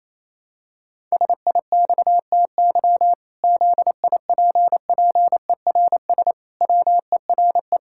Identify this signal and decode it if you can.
Morse code, heard as a keyed tone, about 28 wpm, 700 Hz. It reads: SIXTY ZIPPERS WERE
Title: Pangram Challenge v2 - 28wpm